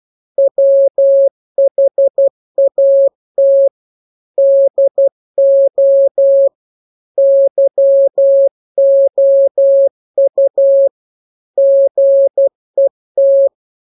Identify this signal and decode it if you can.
Morse code, heard as a keyed tone, about 12 wpm, 550 Hz. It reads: WHAT DO YOU GET
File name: jokes / joke 12 12 0000